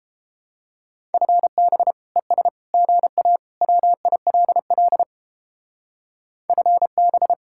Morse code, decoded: FB ES GA WILL FB

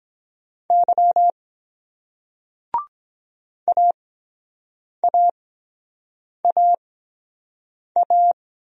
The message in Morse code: Y AAAA

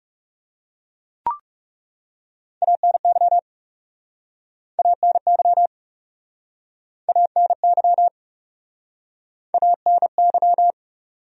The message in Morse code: ANY ANY ANY ANY

